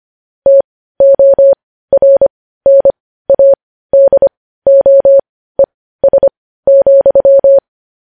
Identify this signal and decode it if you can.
Morse code, heard as a keyed tone, about 25 wpm, spaced as for 15 wpm, 550 Hz. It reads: TORNADOES,
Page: news headlines in morse code at 15 wpm